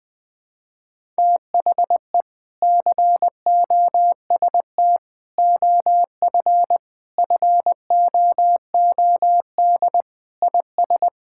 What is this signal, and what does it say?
Morse code, heard as a keyed tone, about 20 wpm, 700 Hz. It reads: THE COST OF FOOD IS
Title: Sentences with 3-Word Phrases - 20wpm